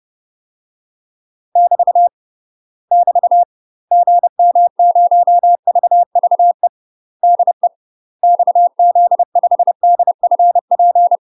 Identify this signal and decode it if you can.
Morse code, heard as a keyed tone, about 30 wpm, 700 Hz. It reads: <BT> <BT> GM0VVE DE XZ5DFP
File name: UK MASTER QSOs TRAINING / 30WPM / UK MASTER QSOs 0017